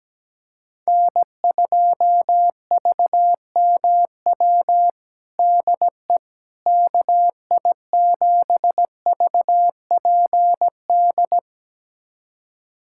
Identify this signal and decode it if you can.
Morse code, heard as a keyed tone, about 17 wpm, 700 Hz. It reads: N2VMW DE KI7VPD
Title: USA Callsign DE Callsign 17wpm